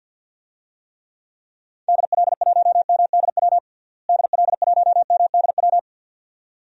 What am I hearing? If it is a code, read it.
Morse code, heard as a keyed tone, about 52 wpm, 700 Hz. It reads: DL1MDW DL1MDW
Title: DXCC-100 Call Signs - 6 Characters - Repeated 2x - 50wpm